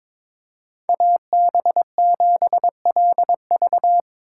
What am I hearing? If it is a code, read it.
Morse code, heard as a keyed tone, about 22 wpm, 700 Hz. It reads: AB7LV
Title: USA Call Signs - 5 Characters - 6x Character Spacing - 22wpm